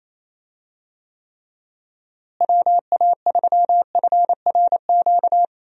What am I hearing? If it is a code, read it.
Morse code, heard as a keyed tone, about 28 wpm, 700 Hz. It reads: WA3FRQ
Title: USA Call Signs - 6 Characters - Repeated 2x - 28wpm